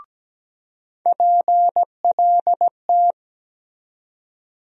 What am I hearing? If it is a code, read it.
Morse code, heard as a keyed tone, about 17 wpm, 700 Hz. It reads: PLT
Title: USA Call Sign Postfixes - 3 Letters 17wpm